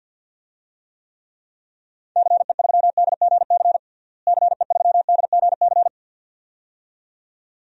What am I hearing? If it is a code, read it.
Morse code, heard as a keyed tone, about 48 wpm, 700 Hz. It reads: CE3DGC CE3DGC
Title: DXCC-100 Call Signs - 6 Characters - Repeated 2x - 50wpm